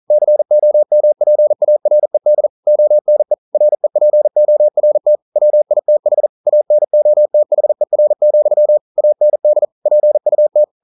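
Morse code, heard as a keyed tone, about 43 wpm, 600 Hz. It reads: COMPARED ONE REPORT WITH ANOTHER, AND PUT
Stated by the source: Morse code practice at 41 wpm, from CW med Gnister.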